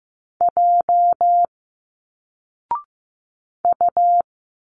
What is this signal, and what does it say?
Morse code, heard as a keyed tone, about 15 wpm, 700 Hz. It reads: J U
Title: Single Letters 15wpm